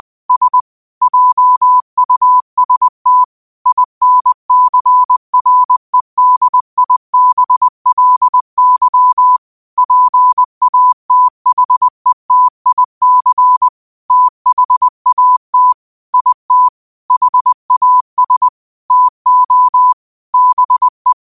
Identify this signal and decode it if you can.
Morse code, heard as a keyed tone, about 20 wpm, 1000 Hz. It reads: S JUST INCREDIBLY PATHETIC THAT IT HAS TO BE
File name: qotdfiles / 20wpm